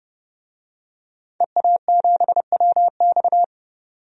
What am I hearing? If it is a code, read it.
Morse code, heard as a keyed tone, about 32 wpm, 700 Hz. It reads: EA7WX
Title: DXCC-100 Call Signs - 5 Characters - 30wpm